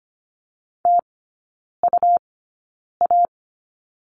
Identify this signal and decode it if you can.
Morse code, heard as a keyed tone, about 25 wpm, spaced as for 9 wpm, 700 Hz. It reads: TUA
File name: Lesson 17e - Letter u - Three Characters 25wpm